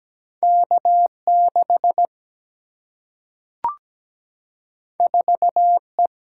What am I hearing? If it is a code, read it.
Morse code, heard as a keyed tone, about 17 wpm, 700 Hz. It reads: K6 4E
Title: Single-Letter Single-Number 17wpm